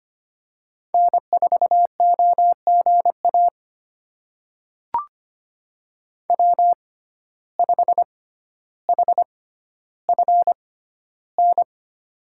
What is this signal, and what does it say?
Morse code, heard as a keyed tone, about 25 wpm, spaced as for 13 wpm, 700 Hz. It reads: N4OGA W5HFN
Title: USA Call Signs - 5 Characters - 6x Character Spacing - 25wpm